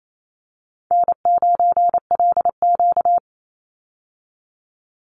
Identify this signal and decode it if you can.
Morse code, heard as a keyed tone, about 28 wpm, 700 Hz. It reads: N9LQ
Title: Easy Call Signs - 28wpm